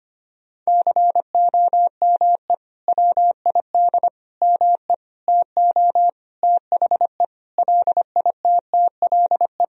Morse code, heard as a keyed tone, about 25 wpm, 700 Hz. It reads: COME WID ME TO THE LITTLE
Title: Sentences from Top 200 Words and Abbreviations 25wpm